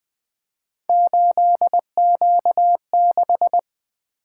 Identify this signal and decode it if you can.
Morse code, heard as a keyed tone, about 20 wpm, 700 Hz. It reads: 8Q6
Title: Three Letter-Number Combinations - 20wpm